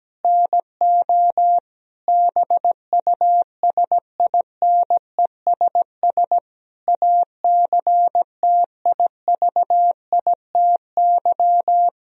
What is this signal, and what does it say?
Morse code, heard as a keyed tone, about 17 wpm, 700 Hz. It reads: NO BUSINESS ACTIVITY
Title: Sentences from Top 1000 Words - Review Part 4 - 17wpm